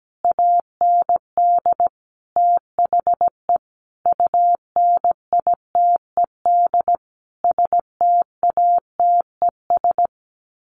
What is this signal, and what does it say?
Morse code, heard as a keyed tone, about 17 wpm, 700 Hz. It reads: AND THE UNITED STATES
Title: Sets of 5 Words Encore 17wpm